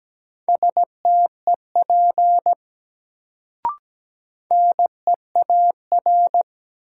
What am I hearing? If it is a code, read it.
Morse code, heard as a keyed tone, about 17 wpm, 700 Hz. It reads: STEP NEAR
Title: Sets of 5 Words Encore - New Word Drill - 17wpm